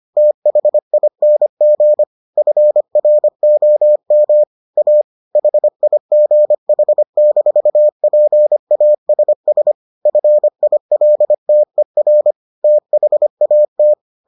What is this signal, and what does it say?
Morse code, heard as a keyed tone, about 25 wpm, 600 Hz. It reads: THING FROM A HIGH-PASS FILTER THAT